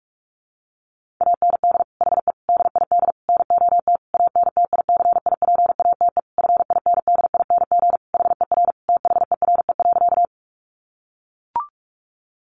Morse code, heard as a keyed tone, about 44 wpm, 700 Hz. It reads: AND HE DID NOT ANTICIPATE FINDING HER THERE.
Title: Sentences from Top 1500 Words 45wpm